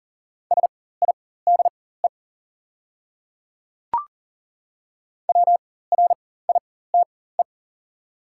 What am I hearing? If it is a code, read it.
Morse code, heard as a keyed tone, about 40 wpm, spaced as for 19 wpm, 700 Hz. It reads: SIDE WRITE
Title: 4x Character Spacing - Top 100 Words - 40wpm